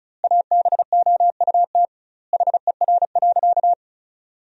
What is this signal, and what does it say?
Morse code, heard as a keyed tone, about 36 wpm, 700 Hz. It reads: ABOUT HER.
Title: 2x Word-Spacing - Sentences from Top 200 Words - 35wpm